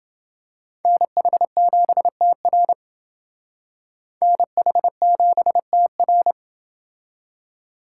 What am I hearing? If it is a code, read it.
Morse code, heard as a keyed tone, about 29 wpm, 700 Hz. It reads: NH7TR NH7TR